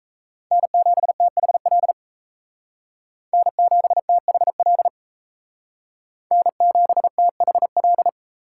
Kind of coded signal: Morse code

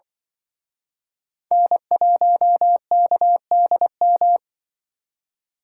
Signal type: Morse code

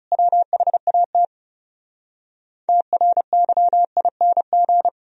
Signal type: Morse code

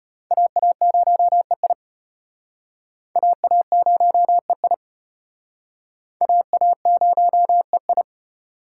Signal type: Morse code

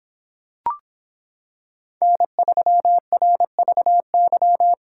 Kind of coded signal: Morse code

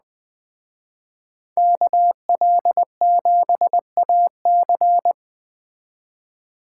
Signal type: Morse code

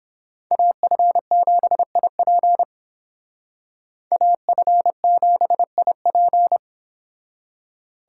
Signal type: Morse code

